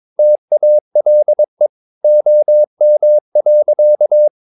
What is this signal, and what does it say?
Morse code, heard as a keyed tone, about 22 wpm, 600 Hz. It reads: TALE OM.
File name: Ferie QRS 22wpm